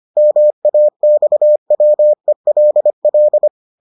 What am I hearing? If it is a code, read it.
Morse code, heard as a keyed tone, about 25 wpm, 600 Hz. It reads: MAXWELL